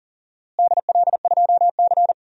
Morse code, heard as a keyed tone, about 37 wpm, 700 Hz. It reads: DL2C
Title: DXCC-100 Call Signs - 4 Characters - Repeated 2x - 40wpm